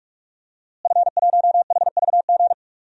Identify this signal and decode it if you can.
Morse code, heard as a keyed tone, about 46 wpm, 700 Hz. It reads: F1HVG
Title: DXCC-100 Call Signs - 5 Characters - Repeated 2x - 45wpm